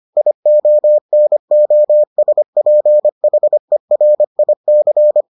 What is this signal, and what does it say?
Morse code, heard as a keyed tone, about 25 wpm, 600 Hz. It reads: IONOSPHERIC